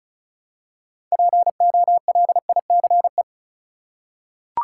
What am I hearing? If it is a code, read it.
Morse code, heard as a keyed tone, about 34 wpm, 700 Hz. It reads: POLICE
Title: Six Letter Words - 35wpm